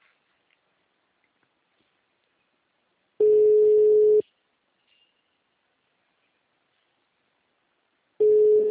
Ringback tone (Europe, 425 Hz).
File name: out- -2051- - - .63152